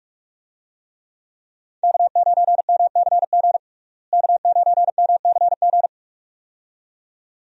Morse code, heard as a keyed tone, about 43 wpm, 700 Hz. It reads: K9MCG K9MCG
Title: USA Call Signs - 5 Characters - Repeated 2x - 45wpm